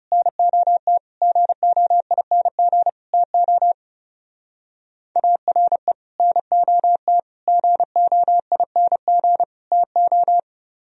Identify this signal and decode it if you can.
Morse code, heard as a keyed tone, about 36 wpm, 700 Hz. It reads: NOT GOING TO ARE NOT GOING TO
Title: Speed-Racing - Sets of 4 Words 30wpm